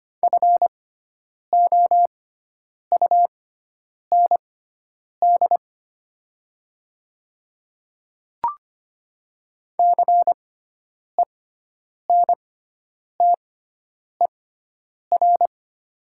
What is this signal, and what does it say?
Morse code, heard as a keyed tone, about 25 wpm, spaced as for 9 wpm, 700 Hz. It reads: FOUND CENTER
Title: 6x Character Spacing - Top 500 Words - 25wpm